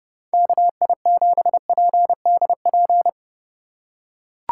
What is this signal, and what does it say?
Morse code, heard as a keyed tone, about 29 wpm, 700 Hz. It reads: KI7PDP